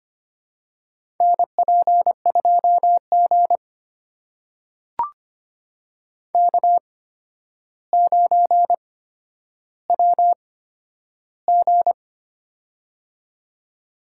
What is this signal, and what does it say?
Morse code, heard as a keyed tone, about 25 wpm, spaced as for 11 wpm, 700 Hz. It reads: NP2G K9WG
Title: USA Call Signs - 4 Characters - 8x Character Spacing - 25wpm